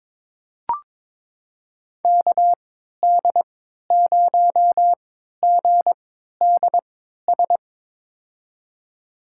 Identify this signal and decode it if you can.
Morse code, heard as a keyed tone, about 22 wpm, spaced as for 13 wpm, 700 Hz. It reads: KD0GDS
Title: USA Call Signs - 6 Characters - 3x Character Spacing - 22wpm